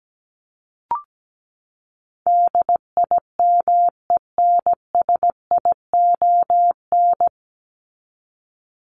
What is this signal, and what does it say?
Morse code, heard as a keyed tone, about 17 wpm, 700 Hz. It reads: DIMENSION
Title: Top 1501-1750 Words 17wpm